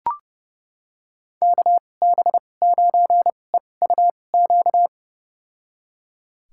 Morse code, heard as a keyed tone, about 30 wpm, spaced as for 22 wpm, 700 Hz. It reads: KB9EUQ
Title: USA Call Signs - 2x Character Spacing - 30wpm